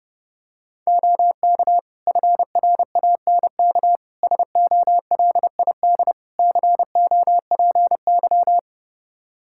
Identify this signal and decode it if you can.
Morse code, heard as a keyed tone, about 30 wpm, 700 Hz. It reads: OK FRANK SOLID COPY
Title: Practice QSO - Exchange 3 - More Info - 30wpm